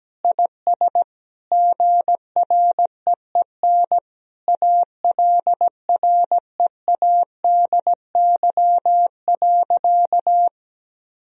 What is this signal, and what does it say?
Morse code, heard as a keyed tone, about 17 wpm, 700 Hz. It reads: IS GREEN ALREADY.
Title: Sentences from Top 1000 Words - Review Part 2 - 17wpm